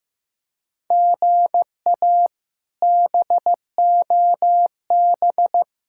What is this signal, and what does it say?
Morse code, heard as a keyed tone, about 15 wpm, 700 Hz. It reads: GA BOB
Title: K1USN SST QSO Practice - 15wpm